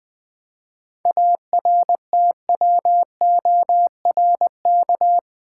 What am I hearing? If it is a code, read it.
Morse code, heard as a keyed tone, about 20 wpm, 700 Hz. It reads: ARTWORK
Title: 7 Letter Words - First 3 Letters Given - 20wpm